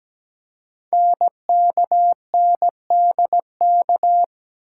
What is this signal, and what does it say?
Morse code, heard as a keyed tone, about 17 wpm, 700 Hz. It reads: NKNDK
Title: Code Groups - 5 Letters - 17wpm